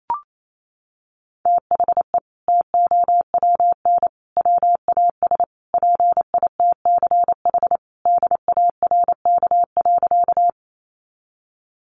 Morse code, heard as a keyed tone, about 28 wpm, 700 Hz. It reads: THE TOWN WAS PITCH DARK.
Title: Sentences from Top 1300 Words 28wpm